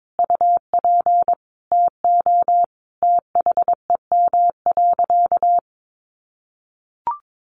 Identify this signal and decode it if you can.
Morse code, heard as a keyed tone, about 22 wpm, 700 Hz. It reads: UP TO THEM.
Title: Sentences from Top 200 Words 22wpm